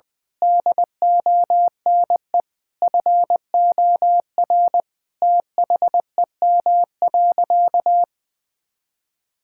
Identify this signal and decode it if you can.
Morse code, heard as a keyed tone, about 20 wpm, 700 Hz. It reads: DONE FOR THEM.
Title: Sentences from Top 1600 Words 20wpm